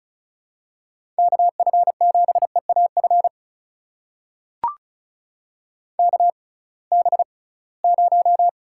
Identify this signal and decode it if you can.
Morse code, heard as a keyed tone, about 34 wpm, spaced as for 22 wpm, 700 Hz. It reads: KF7EAF KB0
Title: USA Call Signs - 6 Characters - 6x Character Spacing - 35wpm